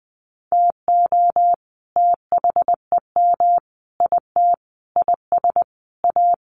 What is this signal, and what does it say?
Morse code, heard as a keyed tone, about 20 wpm, 700 Hz. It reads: TO THEM IT IS A
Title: Sentences from Top 1400 Words 20wpm